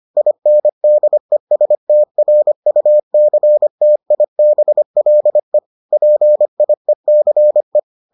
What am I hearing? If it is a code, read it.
Morse code, heard as a keyed tone, about 25 wpm, 600 Hz. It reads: INDESTRUCTIBLE PIECE